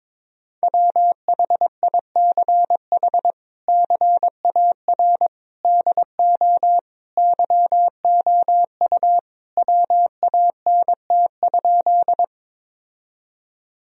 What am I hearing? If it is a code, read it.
Morse code, heard as a keyed tone, about 22 wpm, 700 Hz. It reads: WHICH CAR DO YOU WANT?
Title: Sentences from Top 1000 Words - Review Part 1 - 22wpm